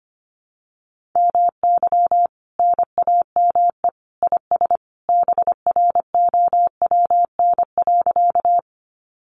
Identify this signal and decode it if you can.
Morse code, heard as a keyed tone, about 25 wpm, 700 Hz. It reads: MY NAME IS BROWN.
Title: Sentences from Top 1200 Words 25wpm